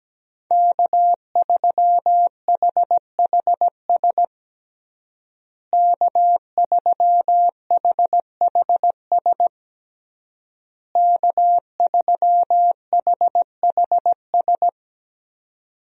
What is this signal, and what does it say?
Morse code, heard as a keyed tone, about 17 wpm, 700 Hz. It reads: K3HHS K3HHS K3HHS
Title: USA Call Signs - 5 Characters - Repeated 3x - 17wpm